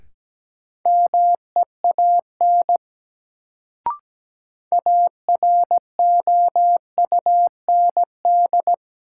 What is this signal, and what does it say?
Morse code, heard as a keyed tone, about 17 wpm, 700 Hz. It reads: MEAN AROUND
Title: Sets of 2 Words - New Word Drill - 17wpm